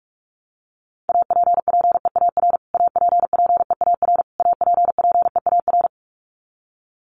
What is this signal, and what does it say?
Morse code, heard as a keyed tone, about 44 wpm, 700 Hz. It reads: APPEAR APPEAR APPEAR
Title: Top 500 Words Repeated 45wpm